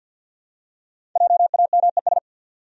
Morse code, heard as a keyed tone, about 52 wpm, 700 Hz. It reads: JAMES
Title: USA Common Names - Max 5 Letters - 50wpm